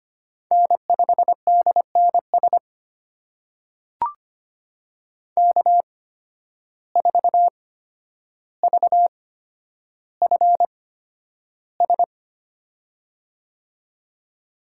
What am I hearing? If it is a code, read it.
Morse code, heard as a keyed tone, about 25 wpm, spaced as for 11 wpm, 700 Hz. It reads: N5DNS K4VFS
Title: USA Call Signs - 5 Characters - 8x Character Spacing - 25wpm